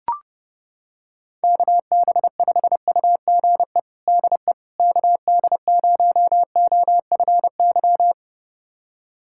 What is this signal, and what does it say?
Morse code, heard as a keyed tone, about 30 wpm, 700 Hz. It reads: KB5UGE DE KD0OFY
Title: USA Callsign DE Callsign 30wpm